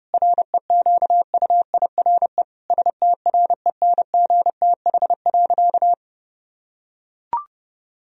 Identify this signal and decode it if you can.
Morse code, heard as a keyed tone, about 32 wpm, 700 Hz. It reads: REQUIRE STRENGTH.
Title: Sentences from Top 1000 Words - Review Part 4 - 30wpm